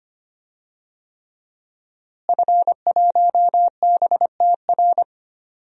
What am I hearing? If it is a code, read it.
Morse code, heard as a keyed tone, about 25 wpm, 700 Hz. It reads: F1BTR
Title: DXCC-100 Call Signs - 5 Characters - Repeated 3x - 25wpm